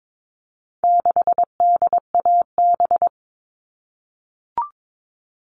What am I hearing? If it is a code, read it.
Morse code, heard as a keyed tone, about 22 wpm, 700 Hz. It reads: 6DAB
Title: All Cut Numbers - 4-Digit 22wpm